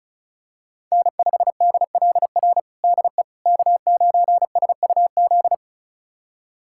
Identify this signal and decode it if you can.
Morse code, heard as a keyed tone, about 36 wpm, 700 Hz. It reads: N5DLR DE K9SUZ
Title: USA Callsign DE Callsign - 5 Characters - 35wpm